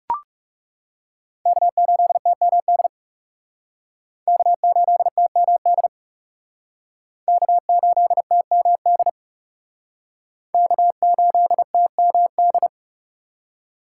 Morse code, heard as a keyed tone, about 37 wpm, 700 Hz. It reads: K8TMD K8TMD K8TMD K8TMD
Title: Speed Racing - USA 1x3 Call Signs 30wpm